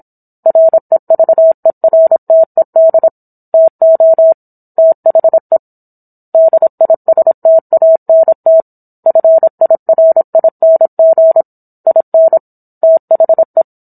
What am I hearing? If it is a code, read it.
Morse code, heard as a keyed tone, about 27 wpm, 650 Hz. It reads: REVERTED TO THE DISTANT FIRING IN THE